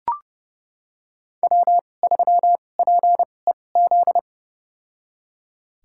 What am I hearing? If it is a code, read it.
Morse code, heard as a keyed tone, about 29 wpm, spaced as for 22 wpm, 700 Hz. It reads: W3PEZ
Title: USA Call Signs - 2x Character Spacing - 30wpm